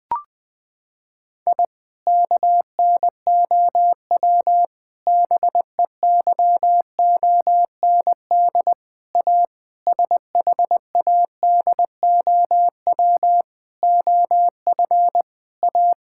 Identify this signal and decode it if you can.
Morse code, heard as a keyed tone, about 20 wpm, 700 Hz. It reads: I KNOW BEYOND A SHADOW OF A
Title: Sentences with 7-Word Phrases - 20wpm